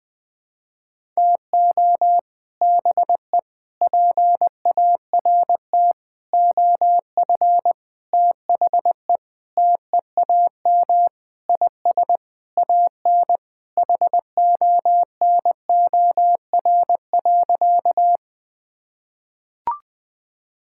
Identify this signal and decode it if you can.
Morse code, heard as a keyed tone, about 20 wpm, 700 Hz. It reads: TO BE PART OF THE TEAM IS AN HONOR.
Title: Sentences with 5-Word Phrases - 20wpm